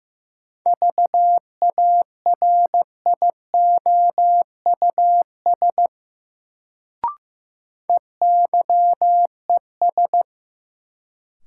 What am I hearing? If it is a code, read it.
Morse code, heard as a keyed tone, about 15 wpm, 700 Hz. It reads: VARIOUS EYES
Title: Sentences from Top 1000 Words - Review Part 4 - New Word Drill - 15wpm